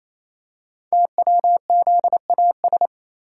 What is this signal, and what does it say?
Morse code, heard as a keyed tone, about 27 wpm, 700 Hz. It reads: TWZAS
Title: Code Groups - 5 Letters - 28wpm